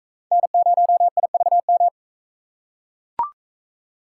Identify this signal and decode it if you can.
Morse code, heard as a keyed tone, about 42 wpm, 700 Hz. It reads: N0IVM